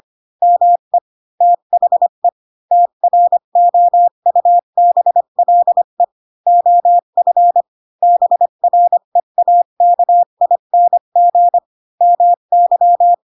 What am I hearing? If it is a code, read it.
Morse code, heard as a keyed tone, about 25 wpm, 700 Hz. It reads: ME THE TROUBLE OF BREAKING MY
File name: Pinocchio / Pinocchio ch5 20 25 ik1hge